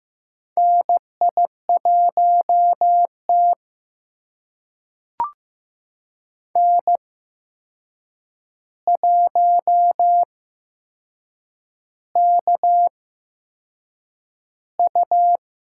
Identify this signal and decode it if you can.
Morse code, heard as a keyed tone, about 15 wpm, spaced as for 6 wpm, 700 Hz. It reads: NI1T N1KU
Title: USA Call Signs - 8x Character Spacing - 15wpm